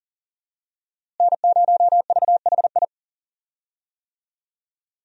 Morse code, heard as a keyed tone, about 40 wpm, 700 Hz. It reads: N0VHI
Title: USA Call Signs - 5 Characters - Repeated 2x - 40wpm